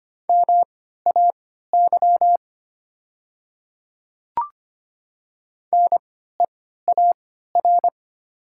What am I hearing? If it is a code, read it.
Morse code, heard as a keyed tone, about 25 wpm, spaced as for 14 wpm, 700 Hz. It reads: MAY NEAR
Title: 3x Character Spacing - Top 200 Words - 25wpm